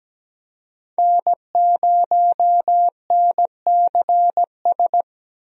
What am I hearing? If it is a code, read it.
Morse code, heard as a keyed tone, about 17 wpm, 700 Hz. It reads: N0NCS
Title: USA Call Signs - 5 Characters - 17wpm